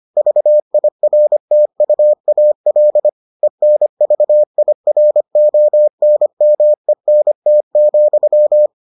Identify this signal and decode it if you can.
Morse code, heard as a keyed tone, about 25 wpm, 600 Hz. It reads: VIRTUAL ENVIRONMENT,